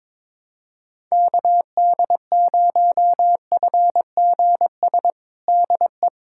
Morse code, heard as a keyed tone, about 22 wpm, 700 Hz. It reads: KD0FGS DE
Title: USA Callsign DE Callsign - 6 Characters - 22wpm